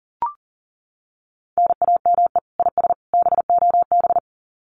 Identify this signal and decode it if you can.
Morse code, heard as a keyed tone, about 40 wpm, 700 Hz. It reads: NAME IS BOB
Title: Common QSO Phrases 40wpm